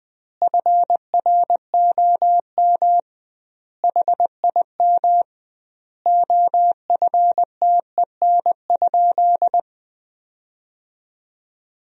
Morse code, heard as a keyed tone, about 20 wpm, 700 Hz. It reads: FROM HIM OFTEN?
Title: 2x Word-Spacing - Sentences from Top 400 Words - 20wpm